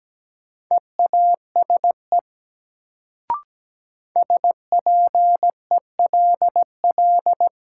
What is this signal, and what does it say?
Morse code, heard as a keyed tone, about 17 wpm, 700 Hz. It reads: EASE SPELL
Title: Sentences from Top 1000 Words - Review Part 1 - New Word Drill - 17wpm